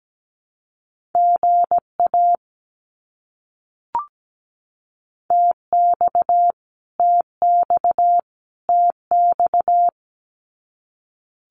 Morse code, heard as a keyed tone, about 17 wpm, 700 Hz. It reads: GA TX TX TX
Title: Abbreviations Repeated 3x - 17wpm